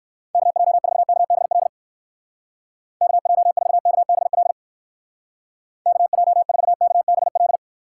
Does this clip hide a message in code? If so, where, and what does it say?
Yes: Morse code from 0.3 to 7.6 s, heard as a keyed tone, about 59 wpm, 700 Hz. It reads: KJ4KBL KJ4KBL KJ4KBL